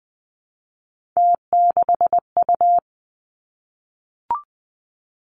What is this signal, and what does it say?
Morse code, heard as a keyed tone, about 20 wpm, 700 Hz. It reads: T6U
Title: Three Challenging Letter-Number Combinations 20wpm